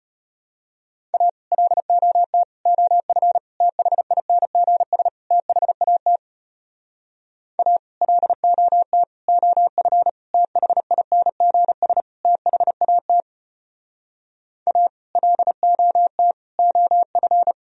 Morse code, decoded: A LOT OF THINGS THAT A LOT OF THINGS THAT A LOT OF

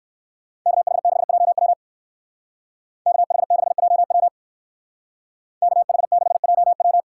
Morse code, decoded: KH6JW KH6JW KH6JW

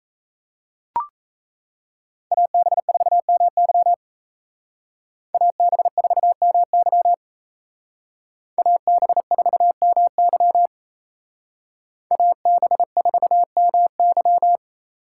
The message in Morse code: AB4MY AB4MY AB4MY AB4MY

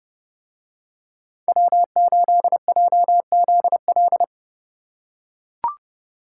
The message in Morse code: W8JZL

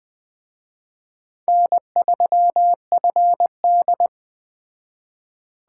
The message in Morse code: N3FD

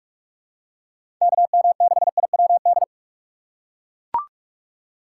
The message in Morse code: KM6IWD

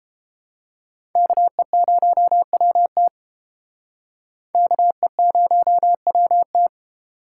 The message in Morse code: KE0WT KE0WT

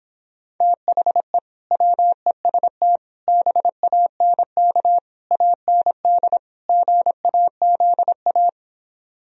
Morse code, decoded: THE WEST BANK AND GAZA

